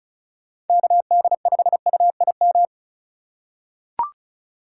Morse code, heard as a keyed tone, about 36 wpm, 700 Hz. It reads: KD5UIM
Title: USA Call Signs - 3x Character Spacing - 35wpm